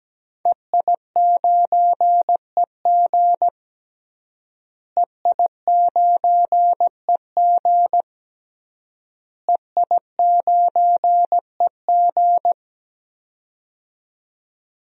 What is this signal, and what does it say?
Morse code, heard as a keyed tone, about 17 wpm, 700 Hz. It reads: EI9EG EI9EG EI9EG
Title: DXCC-100 Call Signs - 5 Characters - Repeated 3x - 17wpm